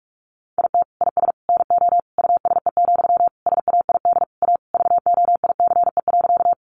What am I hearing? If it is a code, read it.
Morse code, heard as a keyed tone, about 44 wpm, 700 Hz. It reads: IT IS NO USE, SAID A VOICE.
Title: Sentences from Top 500 Words 45wpm